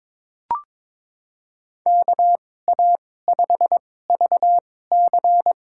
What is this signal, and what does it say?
Morse code, heard as a keyed tone, about 22 wpm, spaced as for 16 wpm, 700 Hz. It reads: KA5VC
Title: USA Call Signs - 2x Character Spacing - 22wpm